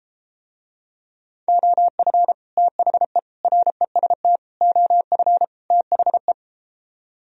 Morse code, heard as a keyed tone, about 33 wpm, 700 Hz. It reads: OF THE REST OF THE